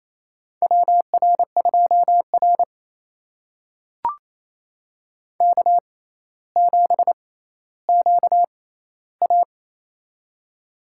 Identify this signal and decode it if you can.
Morse code, heard as a keyed tone, about 28 wpm, spaced as for 14 wpm, 700 Hz. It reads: WR2R K7QA
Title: USA Call Signs - 4 Characters - 6x Character Spacing - 28wpm